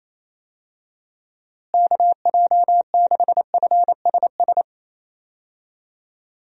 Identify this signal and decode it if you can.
Morse code, heard as a keyed tone, about 28 wpm, 700 Hz. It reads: KJ6FSS
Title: USA Call Signs - 6 Characters - Repeated 2x - 28wpm